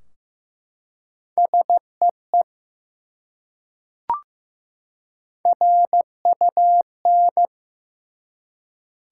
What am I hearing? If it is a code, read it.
Morse code, heard as a keyed tone, about 15 wpm, 700 Hz. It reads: SEE RUN
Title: Three Letter Words - 15wpm